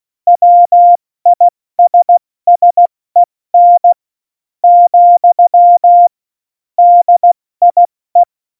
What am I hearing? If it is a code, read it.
Morse code, heard as a keyed tone, about 16 wpm, spaced as for 13 wpm, 700 Hz. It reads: WISSEN, DIE
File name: Morsekurs / Witze / Witz 25. Juni 2019, 70 F80